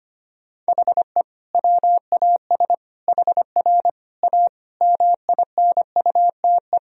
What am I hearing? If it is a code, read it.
Morse code, heard as a keyed tone, about 25 wpm, 700 Hz. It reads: HE WAS HR A MINUTE
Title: Sentences from Top 500 Words and Abbreviations 25wpm